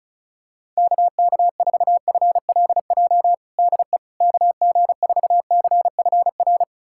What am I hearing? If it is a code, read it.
Morse code, heard as a keyed tone, about 36 wpm, 700 Hz. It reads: KK4FLJ DE KG4CFR
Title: USA Callsign DE Callsign - 6 Characters - 35wpm